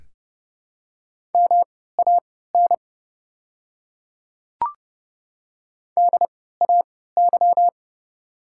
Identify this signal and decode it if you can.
Morse code, heard as a keyed tone, about 31 wpm, spaced as for 17 wpm, 700 Hz. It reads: MAN DAY
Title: 3x Character Spacing - Top 200 Words - 30wpm